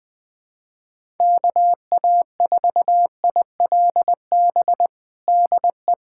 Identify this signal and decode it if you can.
Morse code, heard as a keyed tone, about 21 wpm, 700 Hz. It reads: KA4ILB DE
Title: USA Callsign DE Callsign 20wpm